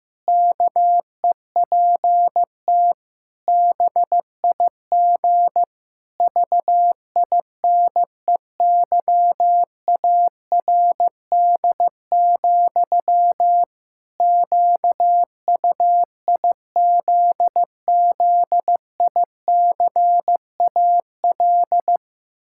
Morse code, heard as a keyed tone, about 15 wpm, 700 Hz. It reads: KEPT BIG VINEYARD, QUIZZICAL
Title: Pangram Challenge v2 - 15wpm